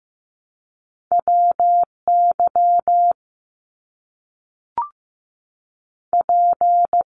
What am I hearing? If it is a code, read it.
Morse code, heard as a keyed tone, about 15 wpm, 700 Hz. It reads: WY P
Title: US States 15wpm